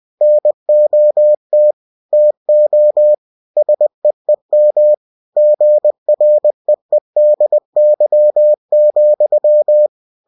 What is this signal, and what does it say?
Morse code, heard as a keyed tone, about 20 wpm, 600 Hz. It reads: NOT TO SEEM GREEDY,
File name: cw-books / winnie-the-pooh / winnie-20wpm-page-0015